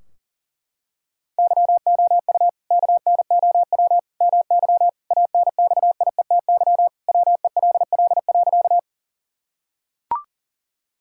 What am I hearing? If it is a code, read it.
Morse code, heard as a keyed tone, about 40 wpm, 700 Hz. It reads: YOU KNOW MY ANXIETY WELL.
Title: Sentences from Top 1600 Words 40wpm